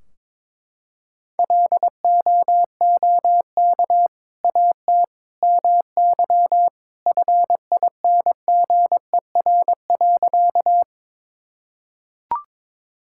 Morse code, heard as a keyed tone, about 22 wpm, 700 Hz. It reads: LOOK AT MY FINGER.
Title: Sentences from Top 1600 Words 22wpm